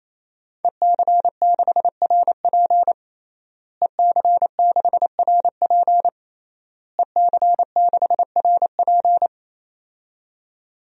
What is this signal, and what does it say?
Morse code, heard as a keyed tone, about 27 wpm, 700 Hz. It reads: EC6RP EC6RP EC6RP
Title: DXCC-100 Call Signs - 5 Characters - Repeated 3x - 28wpm